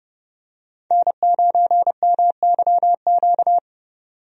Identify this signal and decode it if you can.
Morse code, heard as a keyed tone, about 32 wpm, 700 Hz. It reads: N9MYQ
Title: USA Call Signs - 5 Characters - 30wpm